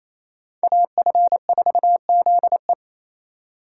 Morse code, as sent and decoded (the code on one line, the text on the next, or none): .- ..-. ....- --.. .
AF4ZE